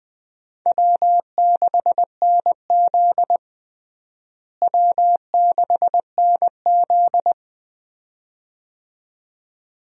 Morse code, heard as a keyed tone, about 20 wpm, 700 Hz. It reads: W6NZ W6NZ
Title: USA Call Signs - 4 Characters - Repeated 2x - 20wpm